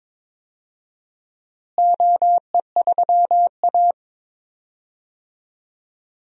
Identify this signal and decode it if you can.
Morse code, heard as a keyed tone, about 23 wpm, 700 Hz. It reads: OE3A
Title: DXCC-100 Call Signs - 4 Characters - Repeated 2x - 22wpm